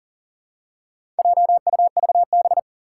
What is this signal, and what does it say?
Morse code, heard as a keyed tone, about 37 wpm, 700 Hz. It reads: JUVB
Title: Four Challenging Letter-Number Combinations 40wpm